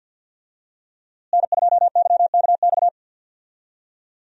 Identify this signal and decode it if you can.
Morse code, heard as a keyed tone, about 50 wpm, 700 Hz. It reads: N2YKX
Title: USA Call Signs - 5 Characters - Repeated 2x - 50wpm